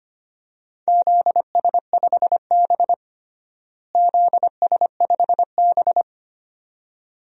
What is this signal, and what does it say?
Morse code, heard as a keyed tone, about 25 wpm, 700 Hz. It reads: ZS5B ZS5B
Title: DXCC-100 Call Signs - 4 Characters - Repeated 2x - 25wpm